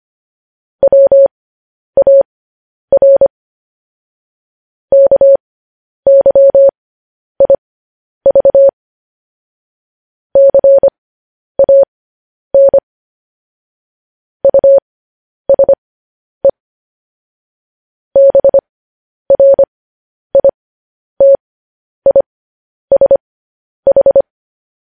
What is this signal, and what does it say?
Morse code, heard as a keyed tone, about 25 wpm, spaced as for 10 wpm, 550 Hz. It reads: WAR KYIV CAN USE BRITISH